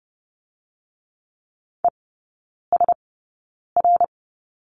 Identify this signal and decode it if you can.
Morse code, heard as a keyed tone, about 30 wpm, spaced as for 9 wpm, 700 Hz. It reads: ESR
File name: course / 30:10wpm / 087 Lesson 15e - Number 5 - Three Characters 30wpm